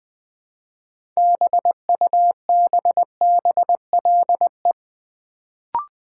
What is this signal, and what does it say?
Morse code, heard as a keyed tone, about 20 wpm, 700 Hz. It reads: BUBBLE
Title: Build A Word - 3 Letters At A Time - 6-Letter Words - 20wpm